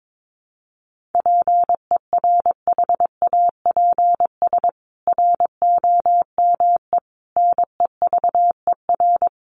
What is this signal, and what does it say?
Morse code, heard as a keyed tone, about 22 wpm, 700 Hz. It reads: PERHAPS ROME NEVER
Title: A Sound Of Thunder 22wpm